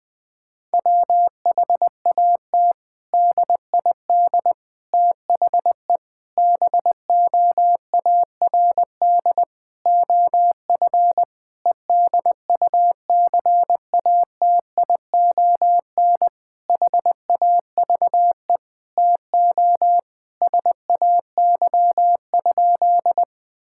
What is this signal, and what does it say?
Morse code, heard as a keyed tone, about 20 wpm, 700 Hz. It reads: WHAT DID THE BOARD OF EDUCATION HAVE TO SAY?
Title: Sentences from Top 1000 Words - Review Part 6 - 20wpm